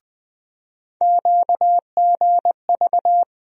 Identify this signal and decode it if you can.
Morse code, heard as a keyed tone, about 20 wpm, 700 Hz. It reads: QGV
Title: USA Call Sign Postfixes - 3 Letters 20wpm